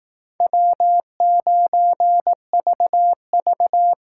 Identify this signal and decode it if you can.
Morse code, heard as a keyed tone, about 18 wpm, 700 Hz. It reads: W9VV
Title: Speed Racing - USA Call Signs 15wpm